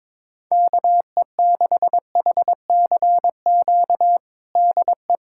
Morse code, heard as a keyed tone, about 22 wpm, 700 Hz. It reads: KE6HCQ DE
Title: USA Callsign DE Callsign 22wpm